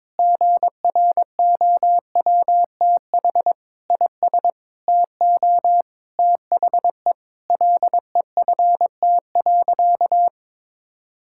Morse code, heard as a keyed tone, about 22 wpm, 700 Hz. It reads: GROWTH IS TO THE LEFT.
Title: Sentences from Top 1000 Words - Review Part 5 - 22wpm